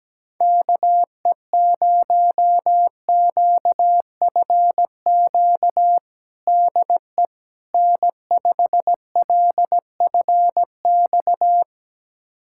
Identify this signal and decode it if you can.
Morse code, heard as a keyed tone, about 17 wpm, 700 Hz. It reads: KE0QFQ DE N5LFX
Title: USA Callsign DE Callsign 17wpm